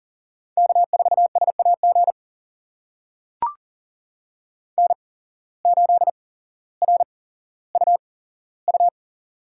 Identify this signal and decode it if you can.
Morse code, heard as a keyed tone, about 40 wpm, spaced as for 17 wpm, 700 Hz. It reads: K4SAG N8RUU
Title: USA Call Signs - 5 Characters - 8x Character Spacing - 40wpm